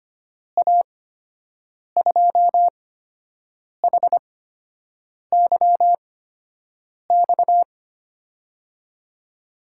Morse code, heard as a keyed tone, about 25 wpm, spaced as for 7 wpm, 700 Hz. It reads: A2HYX
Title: USA Call Signs - 6 Characters - 8x Character Spacing - 25wpm